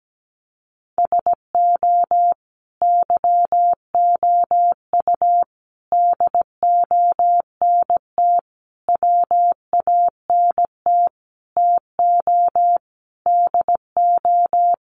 Morse code, heard as a keyed tone, about 17 wpm, 700 Hz. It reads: SO YOU DONT WANT TO DO
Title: Sentences from Top 1000 Words 17wpm